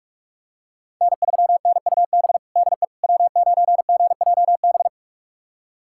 Morse code, heard as a keyed tone, about 46 wpm, 700 Hz. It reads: N3NUB DE W9GJB
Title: USA Callsign DE Callsign - 5 Characters - 45wpm